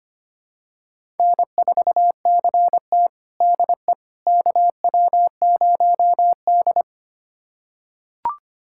Morse code, heard as a keyed tone, about 25 wpm, 700 Hz. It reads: N4CT DE KW0D
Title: USA Callsign DE Callsign - 4 Characters - 25wpm